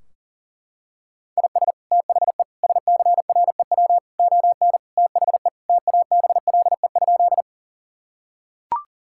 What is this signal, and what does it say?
Morse code, heard as a keyed tone, about 40 wpm, 700 Hz. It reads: IS THE SCREW ON THE TABLE?
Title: Sentences from Top 1600 Words 40wpm